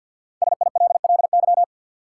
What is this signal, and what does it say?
Morse code, heard as a keyed tone, about 47 wpm, 700 Hz. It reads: SILLY
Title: Five Letter Words - 50wpm